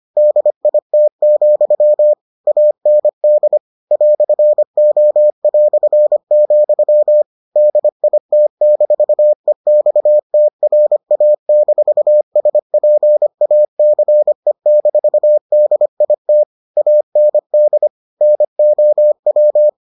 Morse code, heard as a keyed tone, about 25 wpm, 600 Hz. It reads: DIT, AND "O", DIT-EXTRA-SPACE-DIT AND NOW